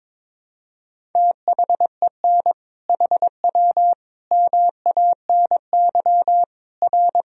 Morse code, heard as a keyed tone, about 22 wpm, 700 Hz. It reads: THEN HW MANY R
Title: Sentences from Top 100 Words and Abbreviations 22wpm